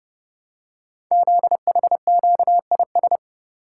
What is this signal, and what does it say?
Morse code, heard as a keyed tone, about 30 wpm, 690 Hz. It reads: ZHQIS
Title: Code Groups - 5 Letters - 30wpm